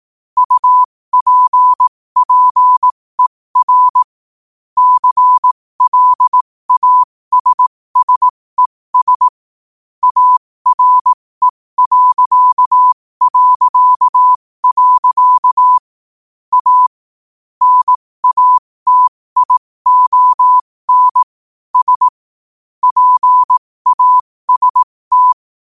Morse code, heard as a keyed tone, about 18 wpm, spaced as for 15 wpm, 1000 Hz. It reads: UPPER CLASSES ARE... A NATION S PAST